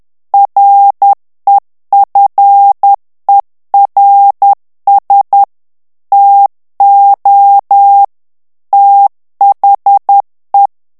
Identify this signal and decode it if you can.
Morse code, heard as a keyed tone, about 10 wpm, 800 Hz. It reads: REFERS TO THE
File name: ANGLAIS texte clair 3 vitesse 10 mots minute